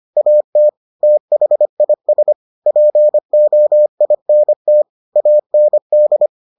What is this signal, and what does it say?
Morse code, heard as a keyed tone, about 25 wpm, 600 Hz. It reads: AT THIS POINT AND